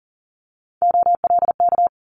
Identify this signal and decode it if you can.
Morse code, heard as a keyed tone, about 42 wpm, 700 Hz. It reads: OLK
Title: Multi-letter Phonemes Quick Fire 40wpm